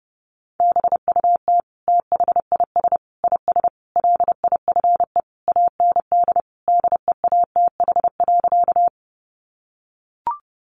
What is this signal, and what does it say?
Morse code, heard as a keyed tone, about 30 wpm, 700 Hz. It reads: BUT THIS IS LIFE AND DEATH.
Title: Sentences from Top 1100 Words 30wpm